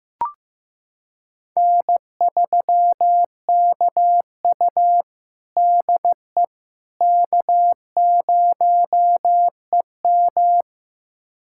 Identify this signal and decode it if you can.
Morse code, heard as a keyed tone, about 15 wpm, 700 Hz. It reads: N3KU DE K0EM
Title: USA Callsign DE Callsign - 4 Characters - 15wpm